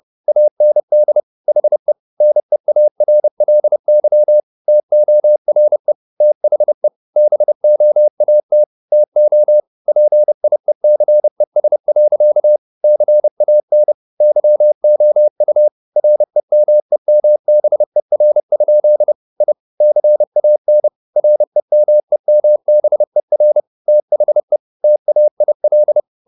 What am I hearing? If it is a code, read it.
Morse code, heard as a keyed tone, about 30 wpm, 600 Hz. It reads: AND HE NEARLY TORE THE BOAT TO PIECES. CAN YOU REMEMBER? I CAN REMEMBER THE TAIL
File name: the old man and the sea / the old man and the sea 1 30wpm